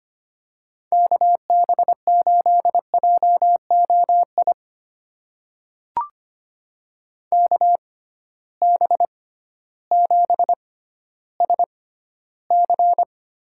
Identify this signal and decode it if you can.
Morse code, heard as a keyed tone, about 25 wpm, spaced as for 13 wpm, 700 Hz. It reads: KB8JOI KB7SC
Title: USA Call Signs - 6x Character Spacing - 25wpm